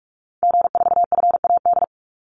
Morse code, heard as a keyed tone, about 44 wpm, 700 Hz. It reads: G4FAD
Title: Call Signs Repeated 2x - 45wpm